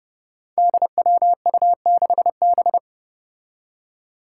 Morse code, heard as a keyed tone, about 30 wpm, 700 Hz. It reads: DWU6B
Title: Five Challenging Letter-Number Combinations 30wpm